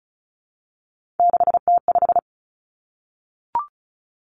Morse code, heard as a keyed tone, about 36 wpm, 700 Hz. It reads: 6T5
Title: Three Challenging Letter-Number Combinations 35wpm